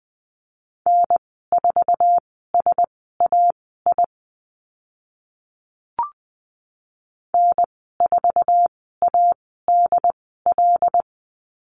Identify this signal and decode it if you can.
Morse code, heard as a keyed tone, about 20 wpm, spaced as for 14 wpm, 700 Hz. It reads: N4SAI N4ADL
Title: 076 Lesson 13g - Letter L - Call Signs 20wpm